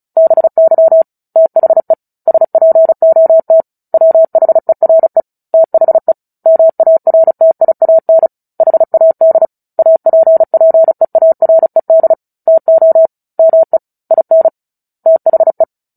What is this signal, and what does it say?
Morse code, heard as a keyed tone, about 34 wpm, 650 Hz. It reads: BY THE SPOT WHERE THE MARTIAN HAD APPEARED TO ME IN THE
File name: Book Two / 09-Chapter 9 - 35 WPM